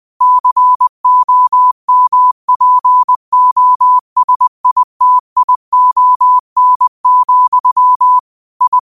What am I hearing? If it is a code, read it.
Morse code, heard as a keyed tone, about 20 wpm, 1000 Hz. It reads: COMPOSITION, I